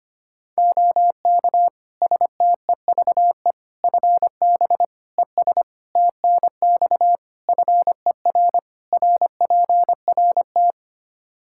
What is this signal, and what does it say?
Morse code, heard as a keyed tone, about 25 wpm, 700 Hz. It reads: OK STEVE FB ES TNX FER RPRT
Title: Practice QSO - Exchange 2 - Info - 25wpm